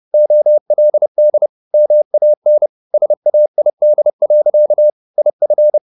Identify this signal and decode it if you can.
Morse code, heard as a keyed tone, about 30 wpm, 600 Hz. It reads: OLD MAN SAID. IF
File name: the old man and the sea / the old man and the sea 1 30wpm